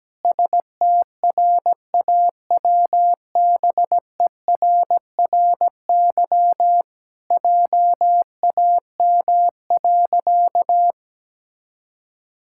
Morse code, heard as a keyed tone, about 17 wpm, 700 Hz. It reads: STRAWBERRY JAM.
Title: Pangram Challenge v2 - 17wpm